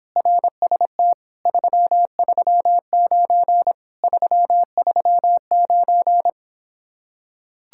Morse code, heard as a keyed tone, about 26 wpm, 700 Hz. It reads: RST 339 339